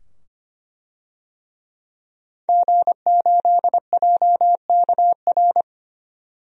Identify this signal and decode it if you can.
Morse code, heard as a keyed tone, about 25 wpm, 700 Hz. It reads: G8JKR
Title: DXCC-100 Call Signs - 5 Characters - 25wpm